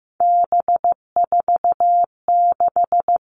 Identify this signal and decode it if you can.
Morse code, heard as a keyed tone, about 15 wpm, 700 Hz. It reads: B46
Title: All Cut Numbers - 3-Digit 15wpm